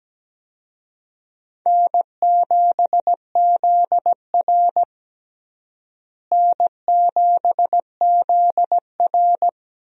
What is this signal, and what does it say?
Morse code, heard as a keyed tone, about 17 wpm, 700 Hz. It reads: N7ZR N7ZR
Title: USA Call Signs - 4 Characters - Repeated 2x - 17wpm